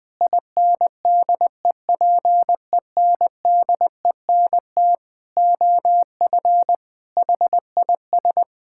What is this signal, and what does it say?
Morse code, heard as a keyed tone, about 20 wpm, 700 Hz. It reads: INDEPENDENT OF HIS
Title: Sentences from Top 1000 Words - Review Part 6 - 20wpm